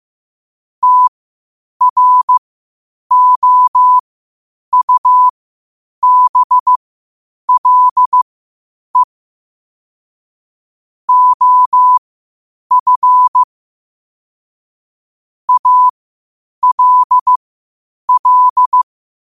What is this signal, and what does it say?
Morse code, heard as a keyed tone, about 15 wpm, spaced as for 8 wpm, 1000 Hz. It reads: TROUBLE OF ALL
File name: qotdfiles / 8wpm